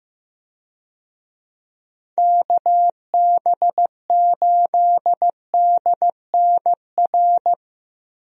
Morse code, heard as a keyed tone, about 15 wpm, 700 Hz. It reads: KB8DNR
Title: USA Call Signs - 6 Characters - Repeated 2x - 15wpm